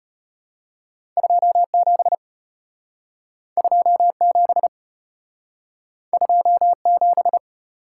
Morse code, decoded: 27 27 27

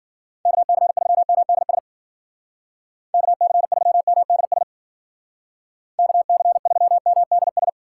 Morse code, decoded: KC3MDS KC3MDS KC3MDS

